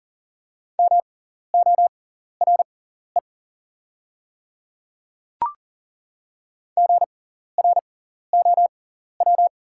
Morse code, heard as a keyed tone, about 40 wpm, spaced as for 14 wpm, 690 Hz. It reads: MORE GROW
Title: 6x Character Spacing - Top 300 Words - 40wpm